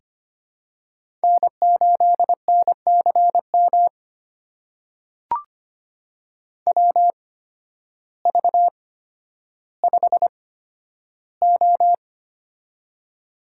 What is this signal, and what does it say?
Morse code, heard as a keyed tone, about 25 wpm, spaced as for 12 wpm, 700 Hz. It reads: N8NCM WV5O
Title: USA Call Signs - 8x Character Spacing - 25wpm